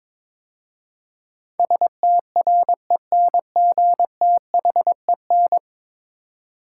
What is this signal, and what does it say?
Morse code, heard as a keyed tone, about 22 wpm, 700 Hz. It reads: STRENGTHEN
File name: practice / 22wpm / 10 Letter Words - First 3 Letters Given - 22wpm